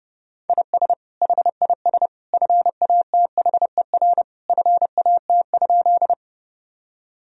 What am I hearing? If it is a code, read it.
Morse code, heard as a keyed tone, about 32 wpm, 700 Hz. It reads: IS HIS FATHER FAT?
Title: Sentences from Top 1000 Words - Review Part 4 - 30wpm